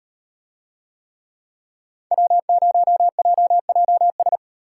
Morse code, heard as a keyed tone, about 38 wpm, 700 Hz. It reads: W0JJS